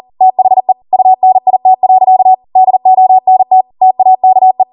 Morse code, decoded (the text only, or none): THE UNIT. DONT TAKE